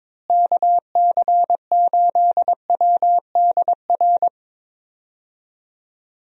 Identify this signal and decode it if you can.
Morse code, heard as a keyed tone, about 22 wpm, 700 Hz. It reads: KC8WDR
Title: USA Call Signs - 6 Characters - Repeated 2x - 22wpm